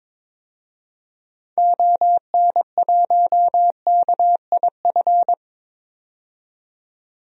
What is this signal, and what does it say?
Morse code, heard as a keyed tone, about 22 wpm, 700 Hz. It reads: ON1KIF
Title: DXCC-100 Call Signs - 6 Characters - 22wpm